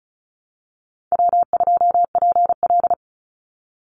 Morse code, heard as a keyed tone, about 34 wpm, 700 Hz. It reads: W2PL
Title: Easy Call Signs Repeated 3x - 35wpm